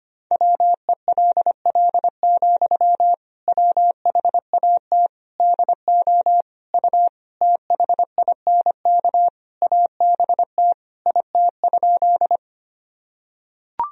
Morse code, heard as a keyed tone, about 25 wpm, 700 Hz. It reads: WELL, WHAT DO U THINK ABT IT?
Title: Sentences from Top 200 Words and Abbreviations 25wpm